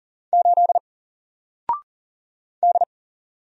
Morse code, heard as a keyed tone, about 40 wpm, 700 Hz. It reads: 8 D
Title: Single Character (Letter:Number:Punctuation:Prosign) 40wpm